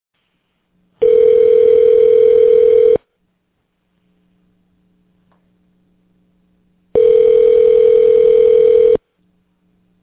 Ringback tone (North America), 440 + 480 Hz.